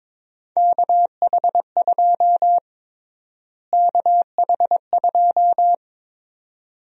Morse code, heard as a keyed tone, about 22 wpm, 700 Hz. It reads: KH2 KH2
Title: USA Call Sign Prefixes - Repeated 2x - 22wpm